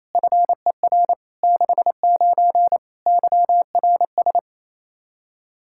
Morse code, heard as a keyed tone, about 28 wpm, 700 Hz. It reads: FER 69 YRS
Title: K1USN SST QSO Practice 28wpm